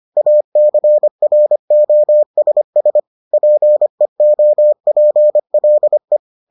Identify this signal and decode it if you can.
Morse code, heard as a keyed tone, about 25 wpm, 600 Hz. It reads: ACROSS PEOPLE